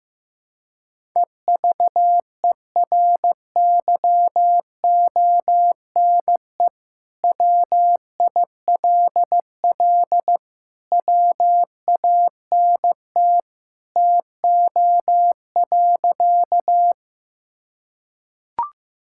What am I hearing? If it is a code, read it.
Morse code, heard as a keyed tone, about 15 wpm, 700 Hz. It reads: EVERYONE WILL WANT TO.
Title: Sentences from Top 1000 Words - Review Part 5 - 15wpm